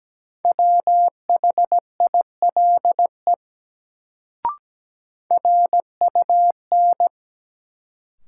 Morse code, heard as a keyed tone, about 17 wpm, 700 Hz. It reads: WHILE RUN
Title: Sentences from Top 300 Words - New Word Drill - 17wpm